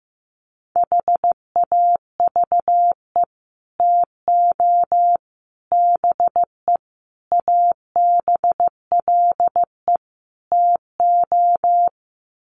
Morse code, heard as a keyed tone, about 15 wpm, 700 Hz. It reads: HAVE TO BE ABLE TO
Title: Sets of 5 Words 15wpm